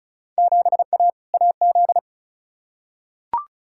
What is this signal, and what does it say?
Morse code, heard as a keyed tone, about 34 wpm, 700 Hz. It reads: 7A AZ
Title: Field Day Exchanges - 35wpm